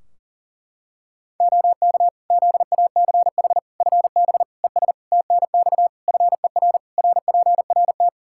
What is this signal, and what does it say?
Morse code, heard as a keyed tone, about 40 wpm, 700 Hz. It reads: OK ZACH FB ES TNX FER RPRT
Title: Practice QSO - Exchange 2 - Info - 40wpm